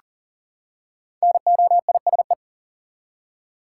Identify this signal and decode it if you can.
Morse code, heard as a keyed tone, about 40 wpm, 700 Hz. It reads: NOISE
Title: Five Letter Words - 40wpm